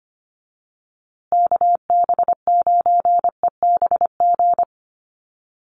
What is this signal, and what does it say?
Morse code, heard as a keyed tone, about 25 wpm, 700 Hz. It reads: KB9EBG
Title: USA Call Signs Repeated 2x 25wpm